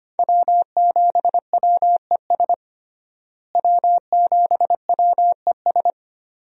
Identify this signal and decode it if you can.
Morse code, heard as a keyed tone, about 25 wpm, 700 Hz. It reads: W7WES W7WES
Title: USA Call Signs - 5 Characters - Repeated 2x - 25wpm